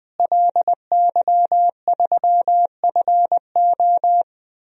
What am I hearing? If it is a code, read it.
Morse code, heard as a keyed tone, about 20 wpm, 700 Hz. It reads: LY3FO
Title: DXCC-100 Call Signs - 5 Characters - Repeated 2x - 20wpm